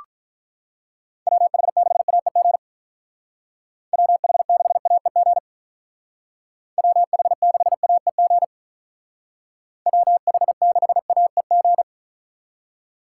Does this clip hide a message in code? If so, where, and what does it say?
Yes: Morse code from 1.3 to 11.8 s, heard as a keyed tone, about 45 wpm, 700 Hz. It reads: WH6AEG WH6AEG WH6AEG WH6AEG